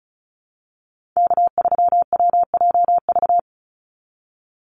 Morse code, heard as a keyed tone, about 34 wpm, 700 Hz. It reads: K3WJV
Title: Call Signs Repeated 3x - 35wpm